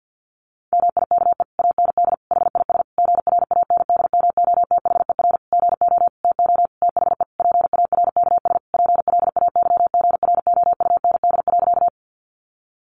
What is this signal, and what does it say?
Morse code, heard as a keyed tone, about 50 wpm, 700 Hz. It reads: MIKE AND HIS GRANDMOTHER GO TO THE PARKS PLAYGROUND.
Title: Additional Easy Phrases 50wpm